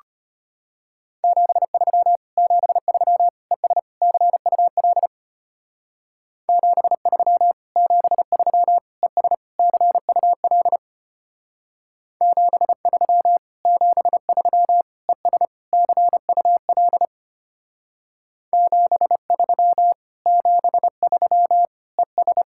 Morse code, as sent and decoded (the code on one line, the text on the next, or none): --... ...-- / --... ...-- / . ... / -.-. ..- .-.. / --... ...-- / --... ...-- / . ... / -.-. ..- .-.. / --... ...-- / --... ...-- / . ... / -.-. ..- .-.. / --... ...-- / --... ...-- / . ...
73 73 ES CUL 73 73 ES CUL 73 73 ES CUL 73 73 ES